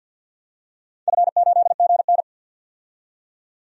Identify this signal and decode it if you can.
Morse code, heard as a keyed tone, about 46 wpm, 700 Hz. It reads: F8GN
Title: DXCC-100 Call Signs - 4 Characters - 50wpm